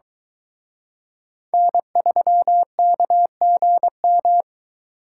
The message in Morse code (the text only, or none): N3KGM